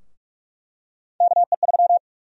Morse code, heard as a keyed tone, about 44 wpm, 700 Hz. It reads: KE3
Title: USA Call Sign Prefixes - 45wpm